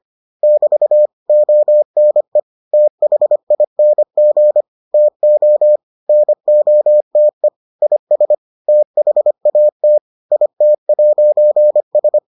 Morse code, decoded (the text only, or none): <BT> ONE THING TO NOTE IS THAT IT'S